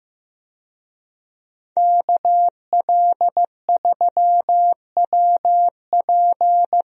Morse code, heard as a keyed tone, about 16 wpm, 700 Hz. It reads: KL3WP
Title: USA Call Signs - 5 Characters - Repeated 2x - 15wpm